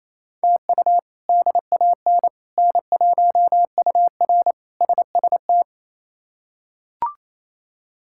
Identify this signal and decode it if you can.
Morse code, heard as a keyed tone, about 28 wpm, 700 Hz. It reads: TU DAN N1UR SST
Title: K1USN SST QSO Practice - 28wpm